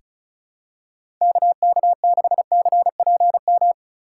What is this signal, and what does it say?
Morse code, heard as a keyed tone, about 36 wpm, 700 Hz. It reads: KK6CPM
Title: USA Call Signs - 6 Characters - 2x Character Spacing - 35wpm